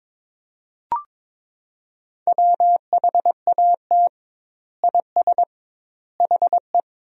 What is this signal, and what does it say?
Morse code, heard as a keyed tone, about 22 wpm, 700 Hz. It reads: WHAT IS HE
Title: 2x Word-Spacing - Sentences from Top 100 Words - 22wpm